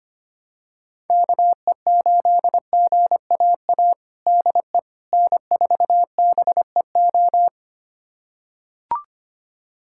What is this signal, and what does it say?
Morse code, heard as a keyed tone, about 25 wpm, 700 Hz. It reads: KE8GAA DE N4BEO
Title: USA Callsign DE Callsign 25wpm